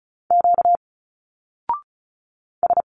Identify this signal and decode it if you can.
Morse code, heard as a keyed tone, about 36 wpm, 700 Hz. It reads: Q S
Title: Single Letters 35wpm